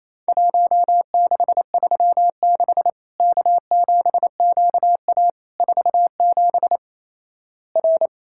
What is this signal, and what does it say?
Morse code, heard as a keyed tone, about 28 wpm, 700 Hz. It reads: 1636 K7QA 47 R
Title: Worked All Europe DX Contest - QTC Exchange - 28wpm